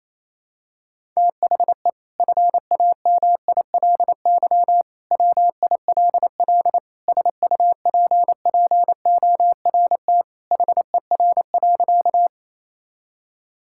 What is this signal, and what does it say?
Morse code, heard as a keyed tone, about 28 wpm, 700 Hz. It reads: THE FAMILY WILL SUPPORT HER.
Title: Sentences from Top 1000 Words - Review Part 5 - 28wpm